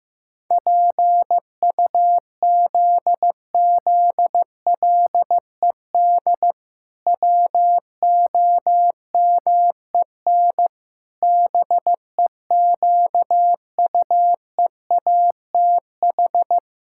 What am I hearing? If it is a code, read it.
Morse code, heard as a keyed tone, about 16 wpm, 700 Hz. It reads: PUZZLED WOMEN BEQUEATH
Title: Pangram Challenge v2 - 15wpm